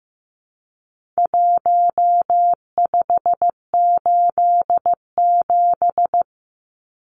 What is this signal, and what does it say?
Morse code, heard as a keyed tone, about 15 wpm, 700 Hz. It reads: 1587
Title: Numbers - Contest Style - 4-Digit 15wpm